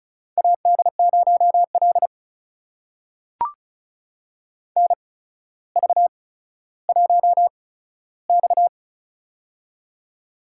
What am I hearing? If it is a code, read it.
Morse code, heard as a keyed tone, about 36 wpm, spaced as for 15 wpm, 700 Hz. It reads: AD0L NV1X
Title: USA Call Signs - 4 Characters - 8x Character Spacing - 35wpm